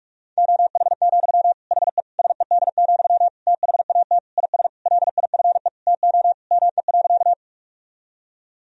Morse code, heard as a keyed tone, about 46 wpm, 700 Hz. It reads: OH, HE SED, THAT IS LIFE TO ME.
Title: Sentences from Top 500 Words and Abbreviations 45wpm